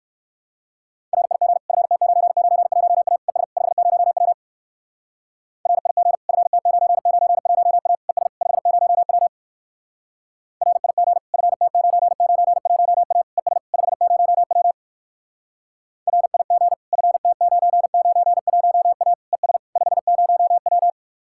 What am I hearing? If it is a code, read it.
Morse code, heard as a keyed tone, about 67 wpm, 700 Hz. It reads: RIG FT991A ES 50W RIG FT991A ES 50W RIG FT991A ES 50W RIG FT991A ES 50W